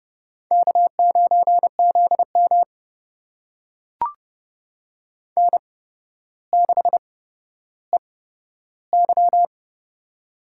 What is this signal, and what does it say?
Morse code, heard as a keyed tone, about 30 wpm, spaced as for 13 wpm, 700 Hz. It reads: K9ZM N6EY
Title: USA Call Signs - 4 Characters - 8x Character Spacing - 30wpm